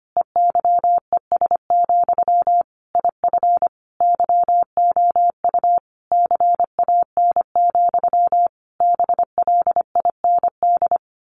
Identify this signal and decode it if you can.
Morse code, heard as a keyed tone, about 25 wpm, 700 Hz. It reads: EYES, IF YOU CAN, BLIND
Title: A Sound Of Thunder 25wpm